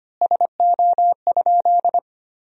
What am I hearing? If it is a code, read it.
Morse code, heard as a keyed tone, about 25 wpm, 700 Hz. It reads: SO?
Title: 3x Word-Spacing - Sentences from Top 200 Words - 25wpm